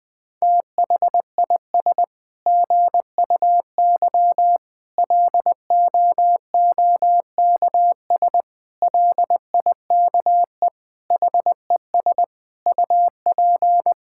Morse code, heard as a keyed tone, about 21 wpm, 700 Hz. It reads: THIS GUY LOOKS LIKE HES UP
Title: Sets of 9 Words - 20wpm